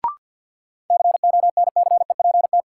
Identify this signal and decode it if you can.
Morse code, heard as a keyed tone, about 50 wpm, 700 Hz. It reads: CONCEPT
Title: Sentences from Top 1100 Words - New Word Drill - 50wpm